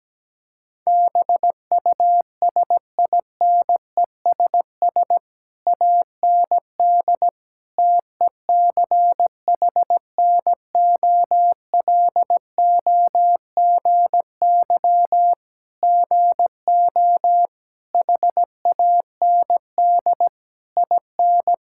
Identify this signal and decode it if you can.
Morse code, heard as a keyed tone, about 17 wpm, 700 Hz. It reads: BUSINESS AND TECHNOLOGY GO HAND IN
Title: Sentences from Top 1000 Words - Review Part 5 - 17wpm